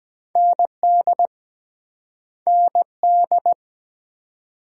Morse code, heard as a keyed tone, about 23 wpm, 700 Hz. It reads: ND ND